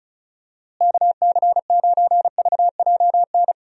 Morse code, heard as a keyed tone, about 36 wpm, 700 Hz. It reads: KC9VJN
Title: USA Call Signs - 6 Characters - Repeated 2x - 35wpm